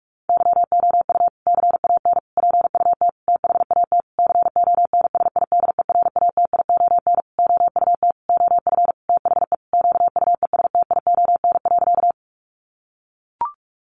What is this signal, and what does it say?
Morse code, heard as a keyed tone, about 44 wpm, 700 Hz. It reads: YOU CAN PUT THAT CONSIDERATION OUT OF THE QUESTION.
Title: Sentences from Top 1500 Words 45wpm